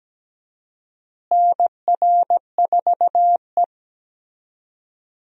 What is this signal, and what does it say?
Morse code, heard as a keyed tone, about 17 wpm, 700 Hz. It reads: NR4E
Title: USA Call Signs - 4 Characters - Repeated 3x - 17wpm